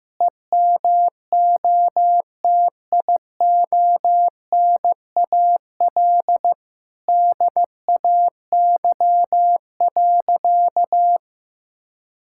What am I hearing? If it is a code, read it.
Morse code, heard as a keyed tone, about 16 wpm, 700 Hz. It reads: EMOTIONAL DAY.
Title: Sentences from Top 1000 Words - Review Part 5 - 15wpm